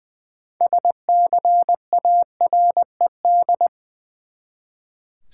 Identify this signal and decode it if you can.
Morse code, heard as a keyed tone, about 20 wpm, 700 Hz. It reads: SCARED
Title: Six Letter Words - 20wpm